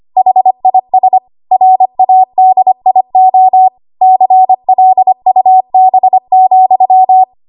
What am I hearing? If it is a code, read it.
Morse code, heard as a keyed tone, about 25 wpm, 750 Hz. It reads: HIS RADIO CLUB,
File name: Morse / 25WPM